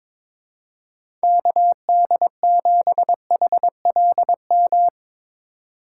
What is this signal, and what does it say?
Morse code, heard as a keyed tone, about 22 wpm, 700 Hz. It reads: KD7HLM
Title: USA Call Signs - 6 Characters - 2x Character Spacing - 22wpm